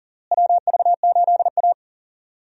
Morse code, heard as a keyed tone, about 40 wpm, 700 Hz. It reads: WV8A
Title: USA Call Signs - 4 Characters - Repeated 3x - 40wpm